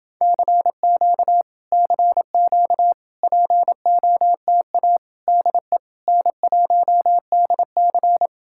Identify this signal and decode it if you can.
Morse code, heard as a keyed tone, about 29 wpm, 700 Hz. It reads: CQ CQ POTA DE N1DC